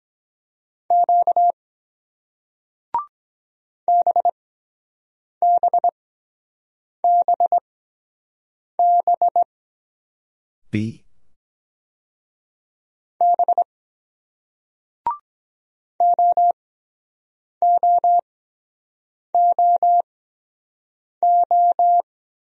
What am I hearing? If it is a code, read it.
Morse code, heard as a keyed tone, about 23 wpm, spaced as for 7 wpm, 700 Hz. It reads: Q BBBB B OOOO